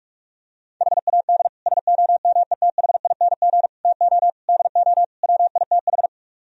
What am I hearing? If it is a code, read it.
Morse code, heard as a keyed tone, about 44 wpm, 700 Hz. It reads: HAD SOMETHING TO DO WITH